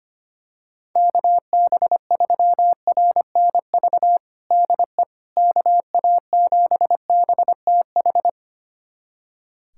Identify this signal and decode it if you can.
Morse code, heard as a keyed tone, about 25 wpm, 700 Hz. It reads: KB3RNV DE KA7BTH
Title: USA Callsign DE Callsign 25wpm